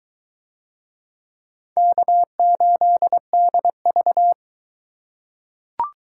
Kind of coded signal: Morse code

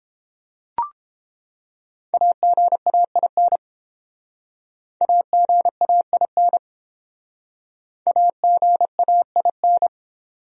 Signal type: Morse code

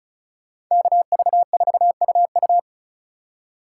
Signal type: Morse code

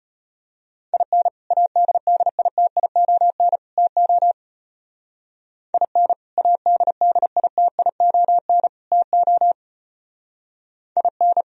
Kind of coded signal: Morse code

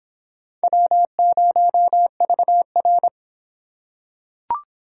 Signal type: Morse code

